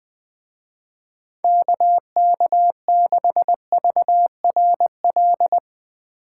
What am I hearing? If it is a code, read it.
Morse code, heard as a keyed tone, about 20 wpm, 700 Hz. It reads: KK6VRL